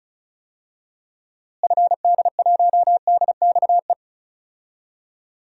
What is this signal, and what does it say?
Morse code, heard as a keyed tone, about 36 wpm, 700 Hz. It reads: FD1DXE
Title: DXCC-100 Call Signs - 6 Characters - Repeated 3x - 35wpm